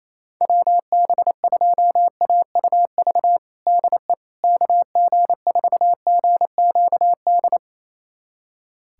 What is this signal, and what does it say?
Morse code, heard as a keyed tone, about 28 wpm, 700 Hz. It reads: WB2AUV DE KG4GQD
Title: USA Callsign DE Callsign - 6 Characters - 28wpm